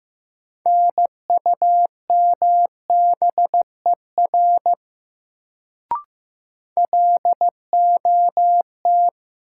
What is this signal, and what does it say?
Morse code, heard as a keyed tone, about 15 wpm, 700 Hz. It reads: NUMBER LOT
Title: Sets of 2 Words - New Word Drill - 15wpm